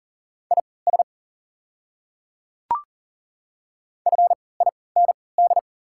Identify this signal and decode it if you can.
Morse code, heard as a keyed tone, about 38 wpm, spaced as for 22 wpm, 690 Hz. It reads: IS FIND
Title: 3x Character Spacing - Top 100 Words - 40wpm